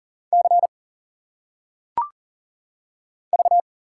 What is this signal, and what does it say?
Morse code, heard as a keyed tone, about 40 wpm, 700 Hz. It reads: C V
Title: Single Letters - Phonetic Pronunciation 40wpm